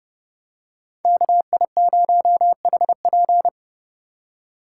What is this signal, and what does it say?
Morse code, heard as a keyed tone, about 30 wpm, 700 Hz. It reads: KI0HP